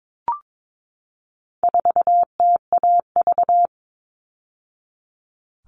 Morse code, heard as a keyed tone, about 22 wpm, 700 Hz. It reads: 4TAV
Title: All Cut Numbers - 4-Digit 22wpm